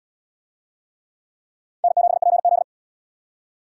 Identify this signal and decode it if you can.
Morse code, heard as a keyed tone, about 70 wpm, 700 Hz. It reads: N7PZ